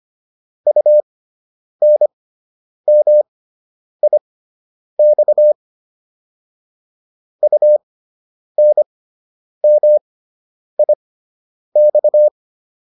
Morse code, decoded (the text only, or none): UNMIX UNMIX